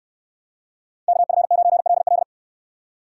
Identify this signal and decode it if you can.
Morse code, heard as a keyed tone, about 64 wpm, 700 Hz. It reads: KF9LL